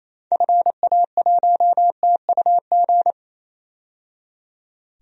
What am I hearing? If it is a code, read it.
Morse code, heard as a keyed tone, about 28 wpm, 700 Hz. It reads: FA1TUG
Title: DXCC-100 Call Signs - 6 Characters - 28wpm